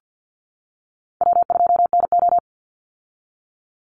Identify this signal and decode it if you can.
Morse code, heard as a keyed tone, about 50 wpm, 700 Hz. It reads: W2NO
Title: Call Signs - 50wpm